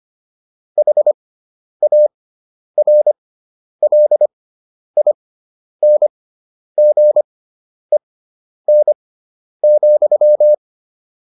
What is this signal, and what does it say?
Morse code, heard as a keyed tone, about 25 wpm, spaced as for 10 wpm, 600 Hz. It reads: HARLINGEN,
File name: QSO304 10